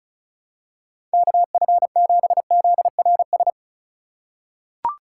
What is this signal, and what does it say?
Morse code, heard as a keyed tone, about 34 wpm, 700 Hz. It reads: KF7ZRS
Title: USA Call Signs - 6 Characters - 3x Character Spacing - 35wpm